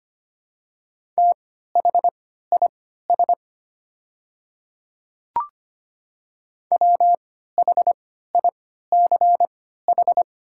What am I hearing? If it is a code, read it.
Morse code, heard as a keyed tone, about 25 wpm, spaced as for 14 wpm, 690 Hz. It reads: THIS WHICH
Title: 3x Character Spacing - Top 300 Words - 25wpm